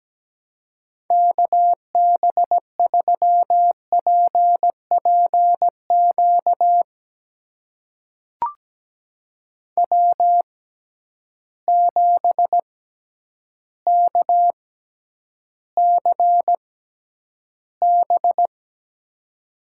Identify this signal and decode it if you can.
Morse code, heard as a keyed tone, about 17 wpm, spaced as for 9 wpm, 700 Hz. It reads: KB3PPQ W7KCB
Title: USA Call Signs - 6x Character Spacing - 17wpm